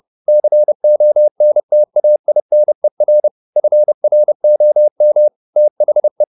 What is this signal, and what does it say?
Morse code, heard as a keyed tone, about 30 wpm, 600 Hz. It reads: CONTAINER FROM THE
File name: the old man and the sea / the old man and the sea 1 30wpm